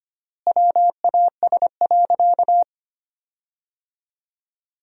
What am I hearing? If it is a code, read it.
Morse code, heard as a keyed tone, about 25 wpm, 700 Hz. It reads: WAS.
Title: 4x Word-Spacing - Sentences from Top 100 Words - 25wpm